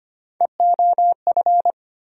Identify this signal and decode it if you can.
Morse code, heard as a keyed tone, about 25 wpm, 700 Hz. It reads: EOF
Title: USA Call Sign Postfixes - 3 Letters 25wpm